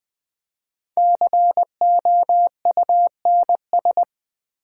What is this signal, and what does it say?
Morse code, heard as a keyed tone, about 20 wpm, 700 Hz. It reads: COUNS
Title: Build A Word - 5 Letters At A Time - 10-Letter Words - 20wpm